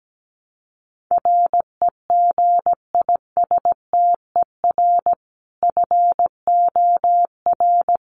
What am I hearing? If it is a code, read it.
Morse code, heard as a keyed tone, about 17 wpm, 700 Hz. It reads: REGISTER FOR
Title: Sentences from Top 1200 Words 17wpm